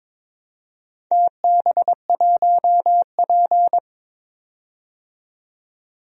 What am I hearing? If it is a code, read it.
Morse code, heard as a keyed tone, about 22 wpm, 700 Hz. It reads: TB1P
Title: DXCC-100 Call Signs - 4 Characters - 22wpm